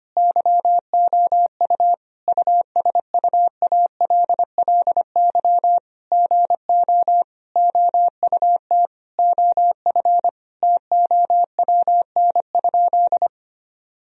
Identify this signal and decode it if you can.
Morse code, heard as a keyed tone, about 25 wpm, 700 Hz. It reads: YOU USUALLY GO OUT OF TOWN?
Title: Sentences from Top 1000 Words - Review Part 3 - 25wpm